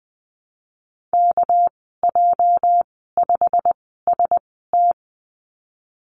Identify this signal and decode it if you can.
Morse code, heard as a keyed tone, about 20 wpm, spaced as for 15 wpm, 700 Hz. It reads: KJ5ST
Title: 219 Lesson 35g - Number 0 - Call Signs 20wpm